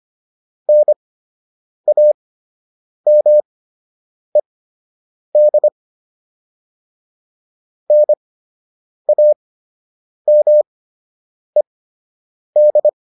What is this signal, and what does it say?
Morse code, heard as a keyed tone, about 25 wpm, spaced as for 8 wpm, 600 Hz. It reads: NAMED NAMED